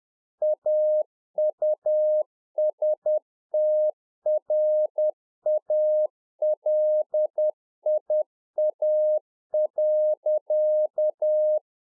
Morse code, heard as a keyed tone, about 10 wpm, 610 Hz. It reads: AUSTRALIA.